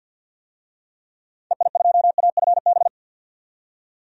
Morse code, heard as a keyed tone, about 49 wpm, 700 Hz. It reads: EI2AFB
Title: DXCC-100 Call Signs - 6 Characters - 50wpm